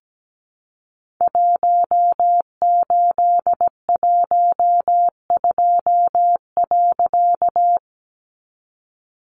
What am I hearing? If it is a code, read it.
Morse code, heard as a keyed tone, about 17 wpm, 700 Hz. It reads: 1812.
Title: A Sound Of Thunder 17wpm